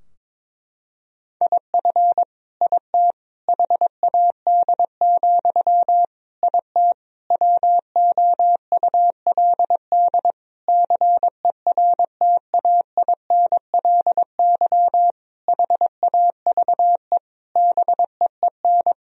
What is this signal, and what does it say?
Morse code, heard as a keyed tone, about 22 wpm, 700 Hz. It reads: IF IT HAD, IT WOULD CERTAINLY HAVE BEEN
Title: Sentences from Top 1000 Words - Review Part 3 - 22wpm